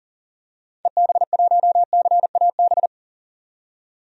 Morse code, heard as a keyed tone, about 40 wpm, 700 Hz. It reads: EB1CAB
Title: DXCC-100 Call Signs - 6 Characters - 40wpm